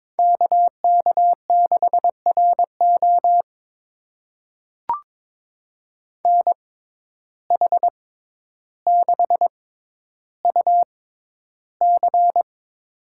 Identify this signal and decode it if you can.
Morse code, heard as a keyed tone, about 22 wpm, spaced as for 11 wpm, 700 Hz. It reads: KK6RO NH6UC
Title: USA Call Signs - 5 Characters - 6x Character Spacing - 22wpm